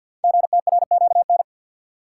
Morse code, heard as a keyed tone, about 48 wpm, 700 Hz. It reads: GTRQN
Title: Code Groups - 5 Letters - 50wpm